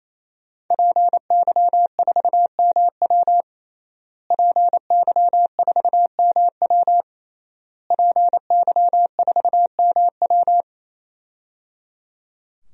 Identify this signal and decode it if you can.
Morse code, heard as a keyed tone, about 27 wpm, 700 Hz. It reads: PY4MW PY4MW PY4MW
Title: DXCC-100 Call Signs - 5 Characters - Repeated 3x - 28wpm